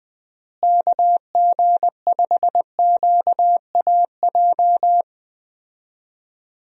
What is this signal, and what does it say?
Morse code, heard as a keyed tone, about 20 wpm, 700 Hz. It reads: KG5QAJ
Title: USA Call Signs - 6 Characters - Repeated 2x - 20wpm